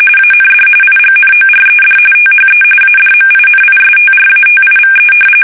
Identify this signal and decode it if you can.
RTTY, about 45.45 baud. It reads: AND IGNORED THE ADMINISTRATIVE
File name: fsk850